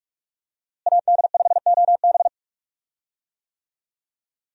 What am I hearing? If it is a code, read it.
Morse code, heard as a keyed tone, about 46 wpm, 700 Hz. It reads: AD5OB
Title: USA Call Signs - 5 Characters - Repeated 2x - 45wpm